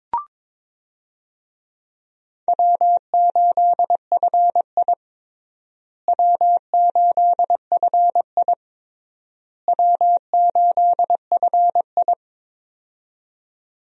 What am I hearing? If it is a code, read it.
Morse code, heard as a keyed tone, about 22 wpm, 700 Hz. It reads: W8FI W8FI W8FI
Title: USA Call Signs - 4 Characters - Repeated 3x - 22wpm